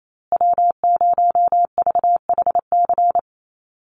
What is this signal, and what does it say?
Morse code, heard as a keyed tone, about 28 wpm, 700 Hz. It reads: W0VHC
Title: USA Call Signs Repeated 3x 28wpm